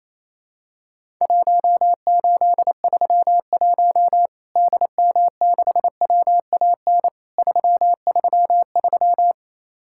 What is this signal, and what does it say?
Morse code, heard as a keyed tone, about 28 wpm, 700 Hz. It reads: 1831 DM6WAN 333
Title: Worked All Europe DX Contest - QTC Exchange - 28wpm